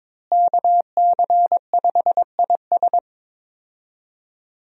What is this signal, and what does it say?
Morse code, heard as a keyed tone, about 22 wpm, 700 Hz. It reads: KC5IS
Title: USA Call Signs - 5 Characters - Repeated 3x - 22wpm